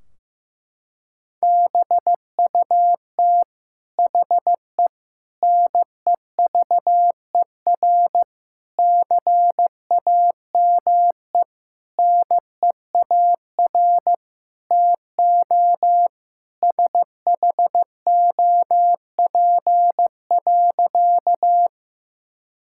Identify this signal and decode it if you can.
Morse code, heard as a keyed tone, about 15 wpm, 700 Hz. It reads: BUT HE NEVER CAME NEAR TO SHOP.
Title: Sentences from Top 1600 Words 15wpm